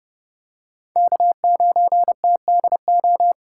Morse code, heard as a keyed tone, about 30 wpm, 700 Hz. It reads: K9TDO